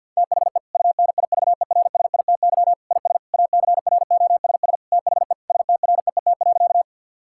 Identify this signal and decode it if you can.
Morse code, heard as a keyed tone, about 50 wpm, 700 Hz. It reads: THE UNIVERSITY IS ACROSS THE STREET.
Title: Sentences from Top 1000 Words - Review Part 5 - 50wpm